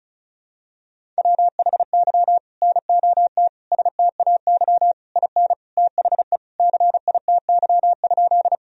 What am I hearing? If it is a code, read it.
Morse code, heard as a keyed tone, about 36 wpm, 700 Hz. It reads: WHY NOT STAY IN THE CITY?
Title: Sentences from Top 1000 Words - Review Part 1 - 35wpm